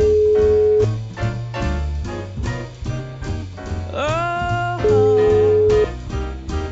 Ringback tone (Europe, 425 Hz).